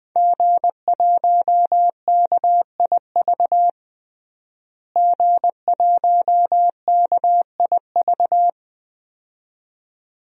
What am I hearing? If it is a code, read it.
Morse code, heard as a keyed tone, about 20 wpm, 700 Hz. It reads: G1KIV G1KIV
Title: DXCC-100 Call Signs - 5 Characters - Repeated 2x - 20wpm